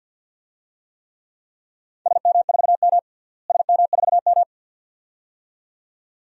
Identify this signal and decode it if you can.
Morse code, heard as a keyed tone, about 53 wpm, 700 Hz. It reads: SM4M SM4M
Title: DXCC-100 Call Signs - 4 Characters - Repeated 2x - 50wpm